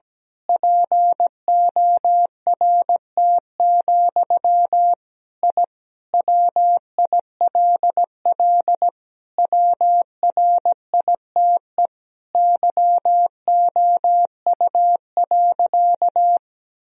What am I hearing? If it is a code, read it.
Morse code, heard as a keyed tone, about 17 wpm, 700 Hz. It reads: PORT, I WILL WRITE YOU.
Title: Sentences from Top 1000 Words - Review Part 1 - 17wpm